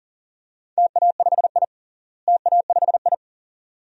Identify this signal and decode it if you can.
Morse code, heard as a keyed tone, about 37 wpm, 700 Hz. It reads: TA5I TA5I
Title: DXCC-100 Call Signs - 4 Characters - Repeated 2x - 40wpm